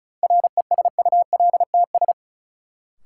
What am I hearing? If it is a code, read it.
Morse code, heard as a keyed tone, about 36 wpm, 700 Hz. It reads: RESULTS
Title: Sets of 4 Words - New Word Drill - 35wpm